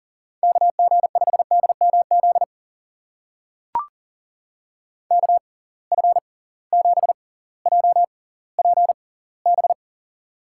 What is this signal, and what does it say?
Morse code, heard as a keyed tone, about 39 wpm, spaced as for 20 wpm, 700 Hz. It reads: KG5DMZ KF7JPB
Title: USA Call Signs - 6 Characters - 6x Character Spacing - 40wpm